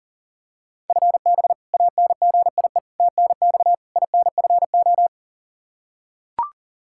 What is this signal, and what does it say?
Morse code, heard as a keyed tone, about 40 wpm, 700 Hz. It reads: FB ANGIE TNX INFO
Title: K1USN SST QSO Practice 40wpm